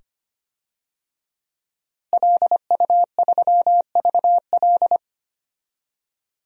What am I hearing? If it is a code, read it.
Morse code, heard as a keyed tone, about 25 wpm, 700 Hz. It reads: LU3VL
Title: DXCC-100 Call Signs - 5 Characters - Repeated 2x - 25wpm